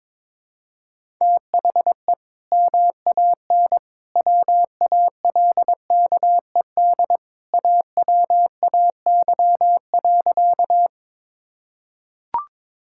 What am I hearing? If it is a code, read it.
Morse code, heard as a keyed tone, about 22 wpm, 700 Hz. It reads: THE MAN WALKED AWAY.
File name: practice / 22wpm / Sentences with 2-Word Phrases - 22wpm